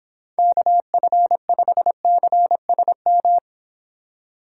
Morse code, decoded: KF5CSM

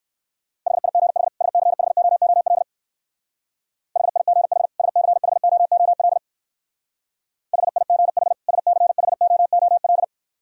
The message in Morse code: HIGH SCHOOL HIGH SCHOOL HIGH SCHOOL

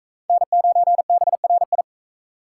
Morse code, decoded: N9BRI

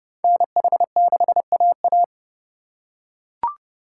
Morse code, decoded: NH6AA

